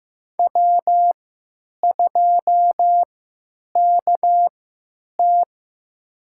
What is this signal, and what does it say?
Morse code, heard as a keyed tone, about 15 wpm, spaced as for 9 wpm, 700 Hz. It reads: W2KT
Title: USA Call Signs - 4 Characters - 3x Character Spacing - 15wpm